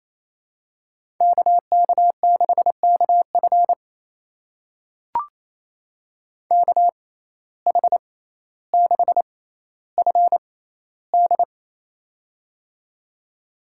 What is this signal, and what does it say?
Morse code, heard as a keyed tone, about 28 wpm, spaced as for 14 wpm, 700 Hz. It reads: KK6KF KH6FD
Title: USA Call Signs - 6x Character Spacing - 28wpm